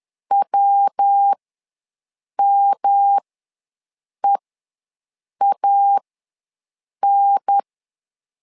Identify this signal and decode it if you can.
Morse code, heard as a keyed tone, about 10 wpm, spaced as for 6 wpm, 800 Hz. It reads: WMEAN
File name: LEZIONI / 3 MWRO / 3 MW ws 10wpm 2213